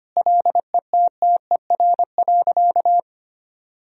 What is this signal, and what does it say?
Morse code, heard as a keyed tone, about 25 wpm, 700 Hz. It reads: LETTER.
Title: Sentences from Top 1000 Words - Review Part 4 - 25wpm